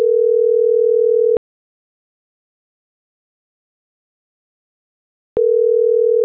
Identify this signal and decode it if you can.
Ringback tone (North America), 440 + 480 Hz.